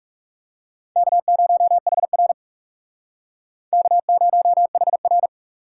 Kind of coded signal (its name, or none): Morse code